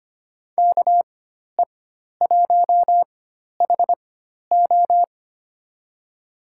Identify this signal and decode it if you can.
Morse code, heard as a keyed tone, about 25 wpm, spaced as for 12 wpm, 700 Hz. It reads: KE1HO
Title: USA Call Signs - 5 Characters - 4x Character Spacing - 25wpm